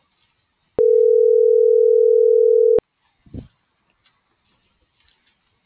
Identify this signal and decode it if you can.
Ringback tone (North America), 440 + 480 Hz.